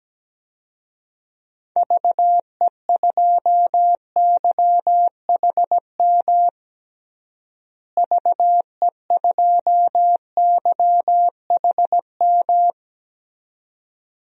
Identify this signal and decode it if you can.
Morse code, heard as a keyed tone, about 17 wpm, 700 Hz. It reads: VE2YHM VE2YHM
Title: DXCC-100 Call Signs - 6 Characters - Repeated 2x - 17wpm